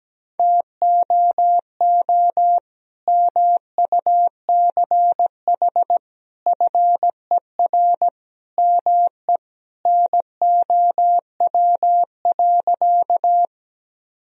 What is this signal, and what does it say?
Morse code, heard as a keyed tone, about 17 wpm, 700 Hz. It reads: TOO MUCH FER ME NOW.
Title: Sentences from Top 200 Words and Abbreviations 17wpm